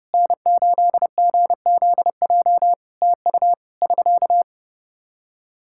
Morse code, heard as a keyed tone, about 32 wpm, 700 Hz. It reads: N8GZJ TU <SK>
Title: K1USN SST QSO Practice 30wpm